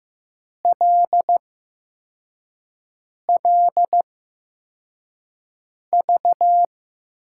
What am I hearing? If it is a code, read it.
Morse code, heard as a keyed tone, about 15 wpm, spaced as for 4 wpm, 700 Hz. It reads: LLV
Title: USA Call Signs - 6 Characters - 8x Character Spacing - 15wpm